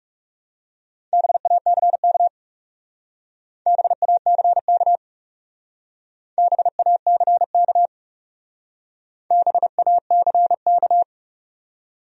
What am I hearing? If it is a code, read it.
Morse code, heard as a keyed tone, about 40 wpm, 700 Hz. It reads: BACK BACK BACK BACK